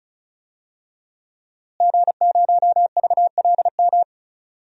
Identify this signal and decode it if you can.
Morse code, heard as a keyed tone, about 34 wpm, 700 Hz. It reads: G0VLM
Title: DXCC-100 Call Signs - 5 Characters - Repeated 2x - 35wpm